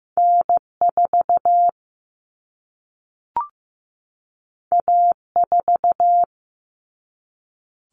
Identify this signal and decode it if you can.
Morse code, heard as a keyed tone, about 15 wpm, 700 Hz. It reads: N4 A4
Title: Common Cut Numbers - 2-Digit 15wpm